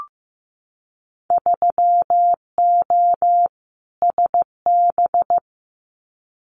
Morse code, heard as a keyed tone, about 15 wpm, 700 Hz. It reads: 3O SB
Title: Winter Field Day Exchanges Encore 15wpm